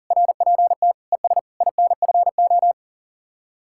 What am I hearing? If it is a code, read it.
Morse code, heard as a keyed tone, about 50 wpm, 700 Hz. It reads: RPT ES INFO
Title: Speed-Racing Common QSO Phrases Encore - 40wpm